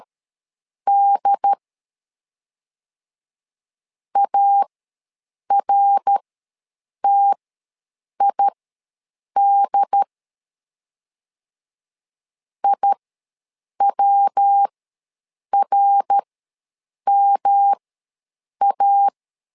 Morse code, heard as a keyed tone, about 12 wpm, spaced as for 7 wpm, 800 Hz. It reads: D ARTID IWRMA
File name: LEZIONI / 3 MWRO / 3 MWR ws 12wpm 1809